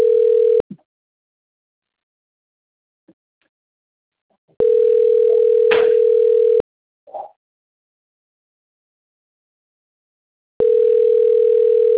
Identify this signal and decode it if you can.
Ringback tone (North America), 440 + 480 Hz.
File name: rg-1600-unknown- - - .23813